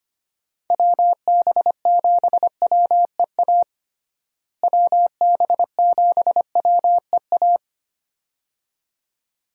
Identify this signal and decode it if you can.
Morse code, heard as a keyed tone, about 25 wpm, 700 Hz. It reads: WB7WEA WB7WEA
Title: USA Call Signs - 6 Characters - Repeated 2x - 25wpm